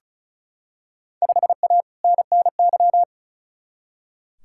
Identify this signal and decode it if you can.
Morse code, heard as a keyed tone, about 36 wpm, 700 Hz. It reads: 5A NNY
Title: Field Day Exchanges - 35wpm